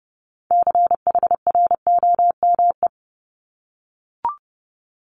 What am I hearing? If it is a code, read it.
Morse code, heard as a keyed tone, about 31 wpm, 700 Hz. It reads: CHROME
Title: Multi-letter Phonemes Repeated 3x 30wpm